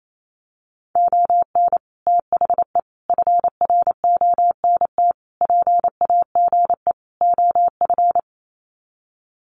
Morse code, heard as a keyed tone, about 28 wpm, 700 Hz. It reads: ON THE FRONT PAGE OF
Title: Sets of 5 Words Encore 28wpm